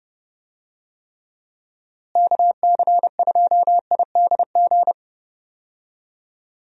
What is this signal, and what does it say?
Morse code, heard as a keyed tone, about 30 wpm, 700 Hz. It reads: KC2IDG
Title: USA Call Signs - 6 Characters - Repeated 2x - 30wpm